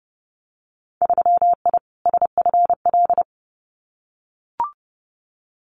Morse code, heard as a keyed tone, about 32 wpm, 700 Hz. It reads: 3I SFL
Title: Winter Field Day Exchanges 30wpm